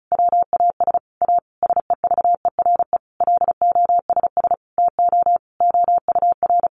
Morse code, heard as a keyed tone, about 34 wpm, 700 Hz. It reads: WAS A SEVERE LOSS TO OUR
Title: Sentences from Top 1400 Words 35wpm